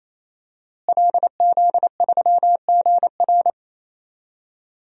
Morse code, heard as a keyed tone, about 27 wpm, 700 Hz. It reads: LZ3GR
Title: DXCC-100 Call Signs - 5 Characters - Repeated 3x - 28wpm